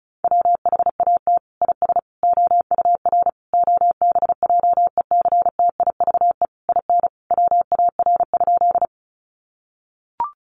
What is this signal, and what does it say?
Morse code, heard as a keyed tone, about 34 wpm, 700 Hz. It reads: WHAT IS OUR OBJECTIVE IN WAR?
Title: Sentences from Top 1200 Words 35wpm